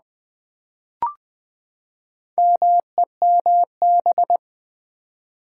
Morse code, heard as a keyed tone, about 20 wpm, 700 Hz. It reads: MEMB
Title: Build A Word - 4 Letters At A Time - 10-Letter Words - 20wpm